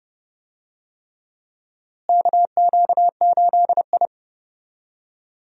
Morse code, heard as a keyed tone, about 32 wpm, 700 Hz. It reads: KQ8I
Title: USA Call Signs - 4 Characters - Repeated 3x - 30wpm